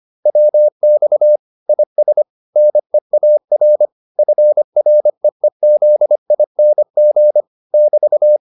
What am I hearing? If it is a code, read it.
Morse code, heard as a keyed tone, about 25 wpm, 600 Hz. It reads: WX IS NEAR FREEZING <BT>
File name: QSO408a 25